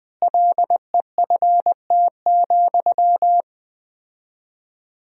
Morse code, heard as a keyed tone, about 20 wpm, 700 Hz. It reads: LEFT,
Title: 4x Word-Spacing - Sentences from Top 400 Words - 20wpm